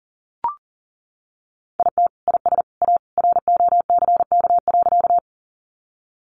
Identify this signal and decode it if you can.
Morse code, heard as a keyed tone, about 40 wpm, 700 Hz. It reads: IT IS A ROCK.
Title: Sentences from Top 400 Words 40wpm